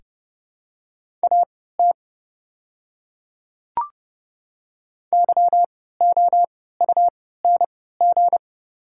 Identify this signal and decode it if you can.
Morse code, heard as a keyed tone, about 29 wpm, spaced as for 17 wpm, 690 Hz. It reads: AT YOUNG
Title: 3x Character Spacing - Top 500 Words - 30wpm